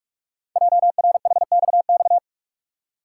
Morse code, heard as a keyed tone, about 44 wpm, 690 Hz. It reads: JRHXX
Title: Code Groups - 5 Letters - 45wpm